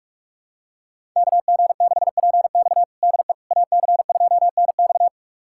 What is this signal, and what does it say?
Morse code, heard as a keyed tone, about 41 wpm, 700 Hz. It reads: KG6PX DE AC2NX
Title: USA Callsign DE Callsign - 5 Characters - 45wpm